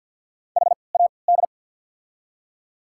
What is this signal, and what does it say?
Morse code, heard as a keyed tone, about 47 wpm, 700 Hz. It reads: H A D
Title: 3x Character Spacing - Top 200 Words - 50wpm